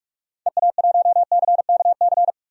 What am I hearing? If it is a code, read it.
Morse code, heard as a keyed tone, about 42 wpm, 700 Hz. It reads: EA1CKC
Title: DXCC-100 Call Signs - 6 Characters - 45wpm